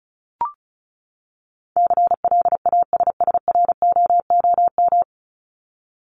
Morse code, heard as a keyed tone, about 34 wpm, 700 Hz. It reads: CLASSROOM
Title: Top 1251-1500 Words 35wpm